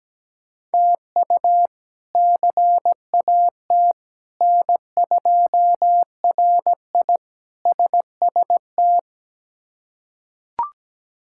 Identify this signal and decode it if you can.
Morse code, heard as a keyed tone, about 17 wpm, 700 Hz. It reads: TU CAT N2RI SST
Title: K1USN SST QSO Practice - 17wpm